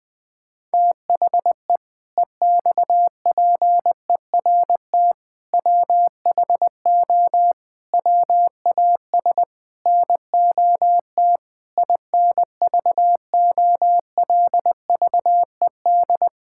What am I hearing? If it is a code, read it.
Morse code, heard as a keyed tone, about 21 wpm, 700 Hz. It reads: THE EXPERT WHO WAS NOT INVOLVED
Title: Sentences with 7-Word Phrases - 20wpm